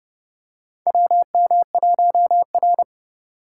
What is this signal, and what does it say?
Morse code, heard as a keyed tone, about 30 wpm, 700 Hz. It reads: WM1R
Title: USA Call Signs - 4 Characters - 4x Character Spacing - 30wpm